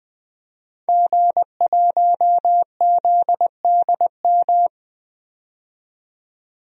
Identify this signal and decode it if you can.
Morse code, heard as a keyed tone, about 21 wpm, 700 Hz. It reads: G1ZDM
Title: DXCC-100 Call Signs - 5 Characters - 20wpm